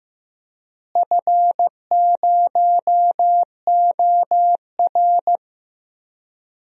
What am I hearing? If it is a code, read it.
Morse code, heard as a keyed tone, about 15 wpm, 700 Hz. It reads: F0OR
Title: DXCC-100 Call Signs - 4 Characters - Repeated 2x - 15wpm